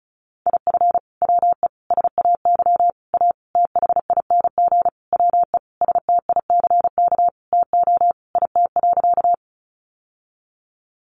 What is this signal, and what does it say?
Morse code, heard as a keyed tone, about 34 wpm, 700 Hz. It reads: IF WE SAY A THING WE STICK TO IT.
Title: Sentences from Top 1200 Words 35wpm